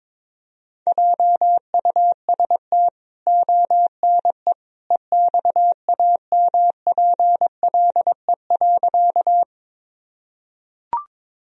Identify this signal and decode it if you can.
Morse code, heard as a keyed tone, about 22 wpm, 700 Hz. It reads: JUST ONE EXAMPLE.
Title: Sentences from Top 1000 Words - Review Part 1 - 22wpm